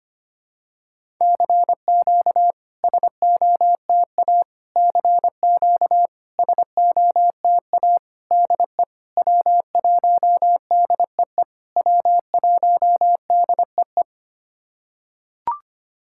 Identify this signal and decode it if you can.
Morse code, heard as a keyed tone, about 25 wpm, 700 Hz. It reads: CQ SOTA CQ SOTA DE W1DEE W1DEE
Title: SOTA QSO Practice 25wpm